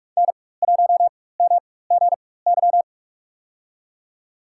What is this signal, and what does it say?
Morse code, heard as a keyed tone, about 44 wpm, spaced as for 21 wpm, 700 Hz. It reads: N1MGY
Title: USA Call Signs - 5 Characters - 4x Character Spacing - 45wpm